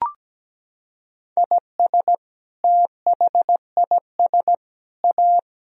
Morse code, heard as a keyed tone, about 17 wpm, 700 Hz. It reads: IS THIS A
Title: Sentences from Top 1000 Words - Review Part 4 - 17wpm